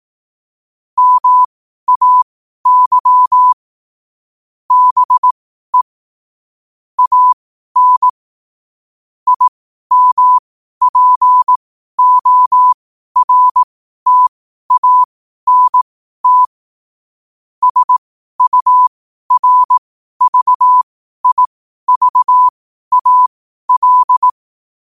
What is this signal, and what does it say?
Morse code, heard as a keyed tone, about 18 wpm, spaced as for 12 wpm, 1000 Hz. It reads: MAY BE AN IMPORTANT SURVIVAL
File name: qotdfiles / 12wpm